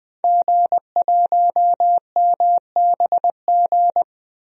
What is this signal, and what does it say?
Morse code, heard as a keyed tone, about 20 wpm, 700 Hz. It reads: G1MBG
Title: DXCC-100 Call Signs - 5 Characters - Repeated 3x - 20wpm